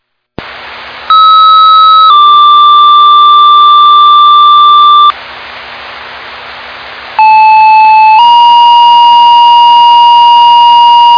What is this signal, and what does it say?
Two-tone page, heard as a one-second tone, then a three-second tone, about 1290 then 1180 Hz, about 870 then 950 Hz. It is heard from 1.1 to 11.2 s.